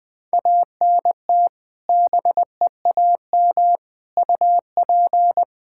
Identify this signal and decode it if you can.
Morse code, heard as a keyed tone, about 20 wpm, 700 Hz. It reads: ANT BEAM UP
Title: K1USN SST QSO Practice 20wpm